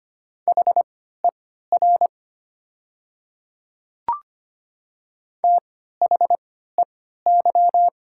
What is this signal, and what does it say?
Morse code, heard as a keyed tone, about 25 wpm, spaced as for 14 wpm, 700 Hz. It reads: HER THEY
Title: 3x Character Spacing - Top 100 Words - 25wpm